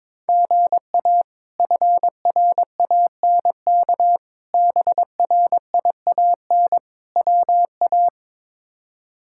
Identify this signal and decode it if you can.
Morse code, heard as a keyed tone, about 22 wpm, 700 Hz. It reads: GA FRANK BRIAN WA
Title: K1USN SST QSO Practice - 22wpm